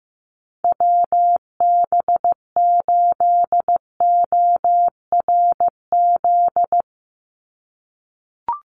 Morse code, heard as a keyed tone, about 15 wpm, 700 Hz. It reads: WB8ORZ
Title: USA Call Signs 15wpm